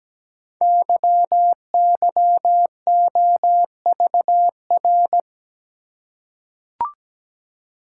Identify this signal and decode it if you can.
Morse code, heard as a keyed tone, about 17 wpm, 700 Hz. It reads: YYOVR
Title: Code Groups - 5 Letters - 17wpm